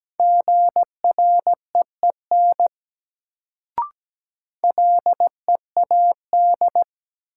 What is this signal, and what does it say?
Morse code, heard as a keyed tone, about 17 wpm, 700 Hz. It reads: GREEN LEAD
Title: Sentences from Top 500 Words - New Word Drill - 17wpm